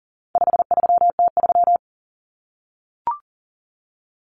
Morse code, heard as a keyed tone, about 40 wpm, 700 Hz. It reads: 53T3
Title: Common Cut Numbers - 4-Digit 40wpm